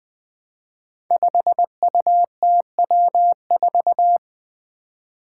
Morse code, heard as a keyed tone, about 20 wpm, 700 Hz. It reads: 5UTW4
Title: Five Challenging Letter-Number Combinations 20wpm